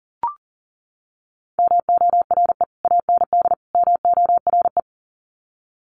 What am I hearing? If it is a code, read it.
Morse code, heard as a keyed tone, about 40 wpm, 700 Hz. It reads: MORE AND MORE
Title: Sets of 3 Words 40wpm